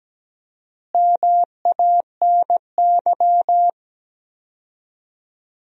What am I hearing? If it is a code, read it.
Morse code, heard as a keyed tone, about 17 wpm, 700 Hz. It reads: MANY
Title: 4x Word-Spacing - Sentences from Top 100 Words - 17wpm